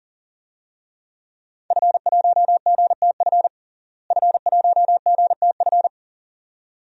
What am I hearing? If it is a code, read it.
Morse code, heard as a keyed tone, about 37 wpm, 700 Hz. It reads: F1GTF F1GTF
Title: DXCC-100 Call Signs - 5 Characters - Repeated 2x - 40wpm